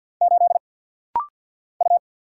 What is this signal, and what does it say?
Morse code, heard as a keyed tone, about 50 wpm, 700 Hz. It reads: 8 U
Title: Single Letter-Number - Rapid-Fire 50wpm